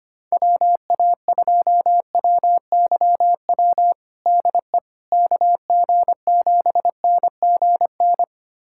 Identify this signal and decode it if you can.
Morse code, heard as a keyed tone, about 25 wpm, 700 Hz. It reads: WA2WYW DE KG7NGN
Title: USA Callsign DE Callsign - 6 Characters - 25wpm